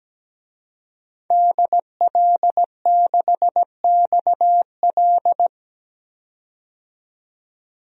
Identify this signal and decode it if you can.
Morse code, heard as a keyed tone, about 17 wpm, 700 Hz. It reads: DL6XL
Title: DXCC-100 Call Signs - 5 Characters - 17wpm